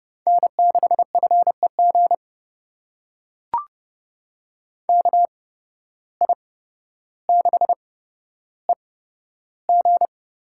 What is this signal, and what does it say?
Morse code, heard as a keyed tone, about 30 wpm, spaced as for 13 wpm, 700 Hz. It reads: N6FEG KI6EG
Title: USA Call Signs - 5 Characters - 8x Character Spacing - 30wpm